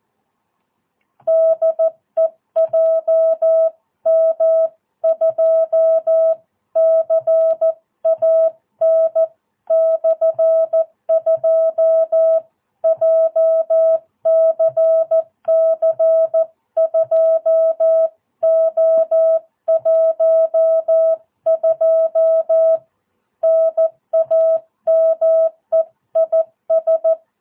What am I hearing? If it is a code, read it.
Morse code, heard as a keyed tone, about 14 wpm, 650 Hz. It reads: DE JM2CAN/2 JCC 2O12 NAME IS
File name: jh8cbh / 2022cw